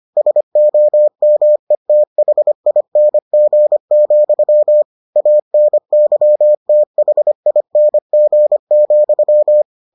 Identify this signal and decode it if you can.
Morse code, heard as a keyed tone, about 25 wpm, 600 Hz. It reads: SOMETHING, ANYTHING,